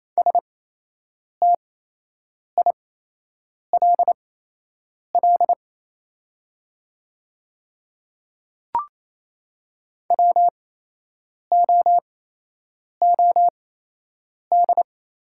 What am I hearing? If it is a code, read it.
Morse code, heard as a keyed tone, about 28 wpm, spaced as for 8 wpm, 700 Hz. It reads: STILL WOOD
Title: 8x Character Spacing - Top 300 Words - 28wpm